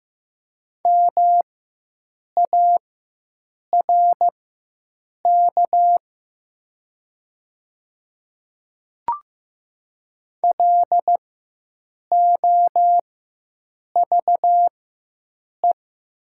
Morse code, heard as a keyed tone, about 15 wpm, spaced as for 7 wpm, 700 Hz. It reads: MARK LOVE
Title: 4x Character Spacing - Top 500 Words - 15wpm